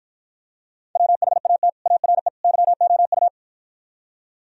Morse code, heard as a keyed tone, about 54 wpm, 700 Hz. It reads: WHAT ARE YOU